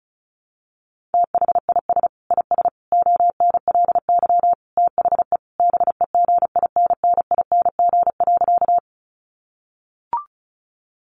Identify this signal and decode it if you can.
Morse code, heard as a keyed tone, about 34 wpm, 700 Hz. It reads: THIS IS ONLY THE BEGINNING.
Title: Sentences from Top 800 Words 35wpm